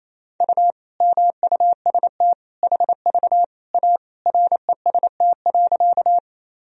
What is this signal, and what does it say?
Morse code, heard as a keyed tone, about 28 wpm, 700 Hz. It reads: U MUST HV A REST.
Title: Sentences from Top 500 Words and Abbreviations 28wpm